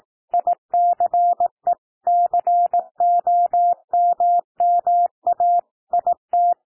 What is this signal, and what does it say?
Morse code, heard as a keyed tone, about 18 wpm, 700 Hz. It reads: ICE COMMA IT
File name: TEXT PASSAGE 18 wpm